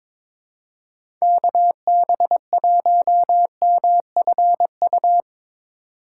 Morse code, heard as a keyed tone, about 23 wpm, 700 Hz. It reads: KB1MFU
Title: USA Call Signs - 6 Characters - 2x Character Spacing - 22wpm